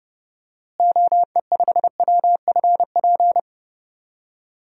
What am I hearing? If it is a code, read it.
Morse code, heard as a keyed tone, about 30 wpm, 700 Hz. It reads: OE5WFP
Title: DXCC-100 Call Signs - 6 Characters - Repeated 2x - 30wpm